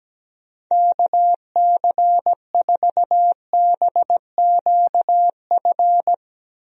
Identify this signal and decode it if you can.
Morse code, heard as a keyed tone, about 17 wpm, 700 Hz. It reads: KC4BQF
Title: USA Call Signs - 6 Characters - Repeated 3x - 17wpm